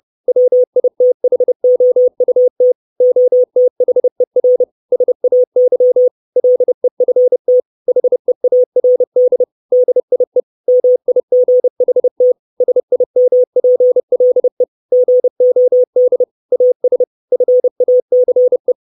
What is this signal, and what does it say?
Morse code, heard as a keyed tone, about 30 wpm, 500 Hz. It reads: WITHOUT OTHER SAY LEFT HEARD DIE MIGHT SIMPLE GOD AS FACE
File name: 632 Words 30WPM